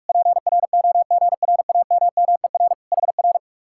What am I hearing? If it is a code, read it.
Morse code, heard as a keyed tone, about 58 wpm, 700 Hz. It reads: PROGRAMMER HR